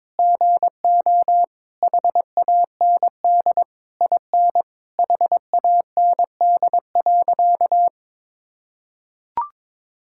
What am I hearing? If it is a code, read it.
Morse code, heard as a keyed tone, about 22 wpm, 700 Hz. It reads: GO HAND IN HAND.
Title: Sentences from Top 1000 Words - Review Part 5 - 22wpm